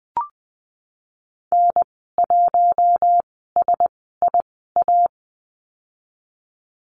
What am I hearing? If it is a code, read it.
Morse code, heard as a keyed tone, about 20 wpm, spaced as for 15 wpm, 700 Hz. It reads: N1SIA
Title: 082 Lesson 14g - Number 2 - Call Signs 20wpm